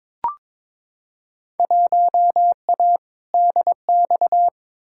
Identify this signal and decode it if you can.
Morse code, heard as a keyed tone, about 22 wpm, 700 Hz. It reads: 1A DX
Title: Field Day Exchanges Encore - 22wpm